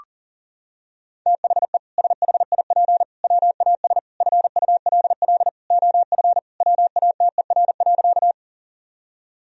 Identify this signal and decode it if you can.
Morse code, heard as a keyed tone, about 40 wpm, 700 Hz. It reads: THE SHIP WAS FULL OF WATER.
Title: Sentences from Top 1000 Words - Review Part 3 - 40wpm